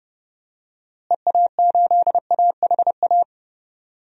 Morse code, heard as a keyed tone, about 30 wpm, 700 Hz. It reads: EA8AHA
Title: DXCC-100 Call Signs - 6 Characters - Repeated 2x - 30wpm